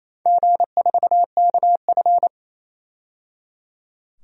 Morse code, heard as a keyed tone, about 27 wpm, 690 Hz. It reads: G4KF
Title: DXCC-100 Call Signs - 4 Characters - 28wpm